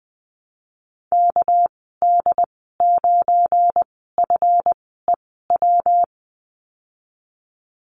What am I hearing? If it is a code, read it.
Morse code, heard as a keyed tone, about 20 wpm, 700 Hz. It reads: K D 9 F E W
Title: 238 Lesson 38g - Letter q - Call Signs 20wpm